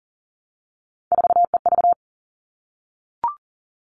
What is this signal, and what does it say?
Morse code, heard as a keyed tone, about 40 wpm, 700 Hz. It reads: 4EV
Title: All Cut Numbers - 3-Digit 40wpm